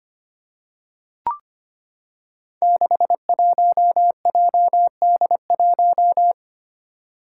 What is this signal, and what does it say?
Morse code, heard as a keyed tone, about 25 wpm, 700 Hz. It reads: 61JD1
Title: Five Challenging Letter-Number Combinations 25wpm